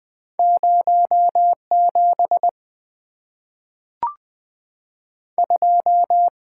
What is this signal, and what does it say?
Morse code, heard as a keyed tone, about 20 wpm, 700 Hz. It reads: 0 7 2
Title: Two Letter-Number Combinations - 20wpm